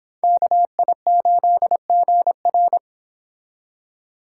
Morse code, heard as a keyed tone, about 29 wpm, 700 Hz. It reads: KI8GR